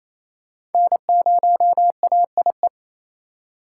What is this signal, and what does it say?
Morse code, heard as a keyed tone, about 28 wpm, 700 Hz. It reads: N0AIE
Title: K1USN SST QSO Practice - 28wpm